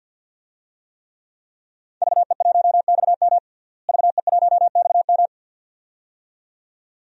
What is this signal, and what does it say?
Morse code, heard as a keyed tone, about 48 wpm, 700 Hz. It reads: VE1XM VE1XM
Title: DXCC-100 Call Signs - 5 Characters - Repeated 2x - 50wpm